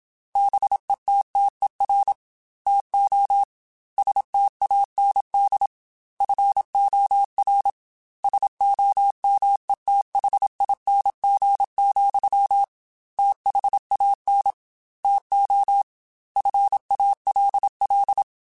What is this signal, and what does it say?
Morse code, heard as a keyed tone, about 27 wpm, 800 Hz. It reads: BETTER TO STAND FOR SOMETHING, THAN TO FALL
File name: Quotes / Quotes-mp3s / Temper @25